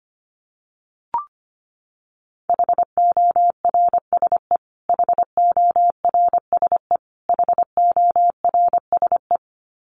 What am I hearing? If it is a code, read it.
Morse code, heard as a keyed tone, about 25 wpm, 700 Hz. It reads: HORSE HORSE HORSE
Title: Top 300 Words Repeated 25wpm